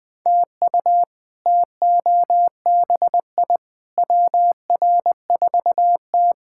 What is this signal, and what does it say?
Morse code, heard as a keyed tone, about 20 wpm, 700 Hz. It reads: TU TOBI WR4T
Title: K1USN SST QSO Practice - 20wpm